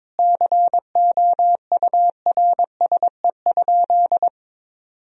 Morse code, heard as a keyed tone, about 22 wpm, 700 Hz. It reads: COURSE?
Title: Sentences from Top 1000 Words - Review Part 2 - 22wpm